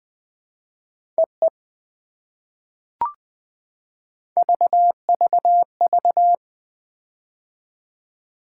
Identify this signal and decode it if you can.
Morse code, heard as a keyed tone, about 19 wpm, 700 Hz. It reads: EE VVV
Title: POTA QSO Practice - 20wpm